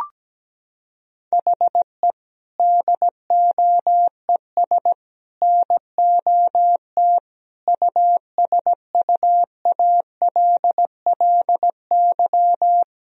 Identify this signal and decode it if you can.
Morse code, heard as a keyed tone, about 17 wpm, 700 Hz. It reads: HE DOES NOT USUALLY
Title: Sentences from Top 1000 Words - Review Part 3 - 17wpm